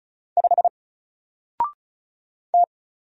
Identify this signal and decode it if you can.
Morse code, heard as a keyed tone, about 36 wpm, 700 Hz. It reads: 5 T
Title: Single Letter-Number Alt 35wpm